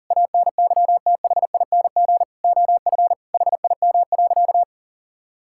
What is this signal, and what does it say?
Morse code, heard as a keyed tone, about 40 wpm, 700 Hz. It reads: ANYTHING OF HIM.
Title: Sentences from Top 1600 Words 40wpm